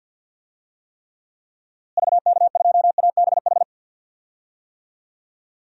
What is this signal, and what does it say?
Morse code, heard as a keyed tone, about 53 wpm, 700 Hz. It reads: VK2ABH
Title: DXCC-100 Call Signs - 6 Characters - Repeated 2x - 50wpm